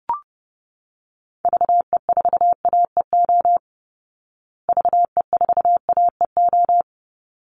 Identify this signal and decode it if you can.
Morse code, heard as a keyed tone, about 30 wpm, 700 Hz. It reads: VE4AEO VE4AEO
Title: Call Signs Repeated 3x - 30wpm